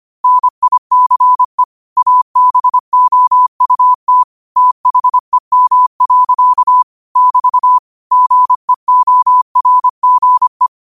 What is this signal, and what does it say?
Morse code, heard as a keyed tone, about 25 wpm, 1000 Hz. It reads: NICE ABOUT THEM. <BT> GEORGE
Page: Quotes for Fri, 15 Aug 2025 in Morse Code at 25 words per minute.